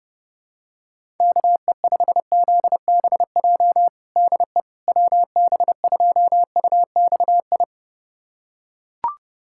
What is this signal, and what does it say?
Morse code, heard as a keyed tone, about 32 wpm, 700 Hz. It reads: KE5ZBJ DE WB2UXI
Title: USA Callsign DE Callsign 30wpm